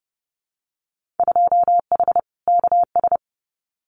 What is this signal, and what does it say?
Morse code, heard as a keyed tone, about 30 wpm, 700 Hz. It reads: 2H KS
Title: Winter Field Day Exchanges Encore 30wpm